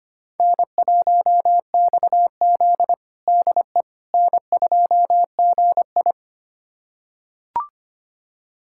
Morse code, heard as a keyed tone, about 25 wpm, 700 Hz. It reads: N1XZ DE N2GI
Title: USA Callsign DE Callsign - 4 Characters - 25wpm